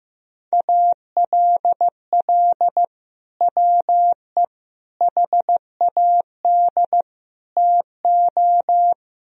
Morse code, heard as a keyed tone, about 15 wpm, 700 Hz. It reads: ALL WE HAD TO
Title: Sentences from Top 1000 Words - Review Part 4 - 15wpm